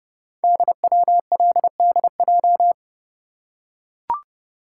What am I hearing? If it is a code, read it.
Morse code, heard as a keyed tone, about 30 wpm, 700 Hz. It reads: DWLDJ
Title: Code Groups - 5 Letters - 30wpm